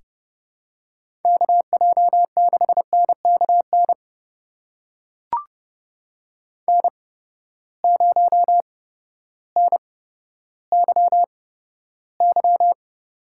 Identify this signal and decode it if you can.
Morse code, heard as a keyed tone, about 31 wpm, spaced as for 14 wpm, 700 Hz. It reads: KJ6NKN N0NYY
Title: USA Call Signs - 8x Character Spacing - 30wpm